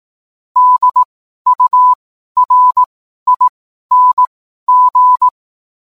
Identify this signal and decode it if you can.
Morse code, heard as a keyed tone, about 18 wpm, spaced as for 12 wpm, 1000 Hz. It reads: DURING
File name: cwblog / cw mp3 directory / mem morse 12